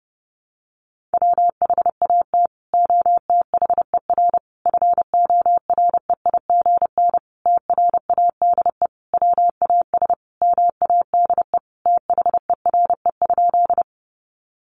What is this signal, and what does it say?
Morse code, heard as a keyed tone, about 30 wpm, 700 Hz. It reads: WHAT OTHER FOREIGN TRADE WAS MADE THERE?
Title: Sentences from Top 1400 Words 30wpm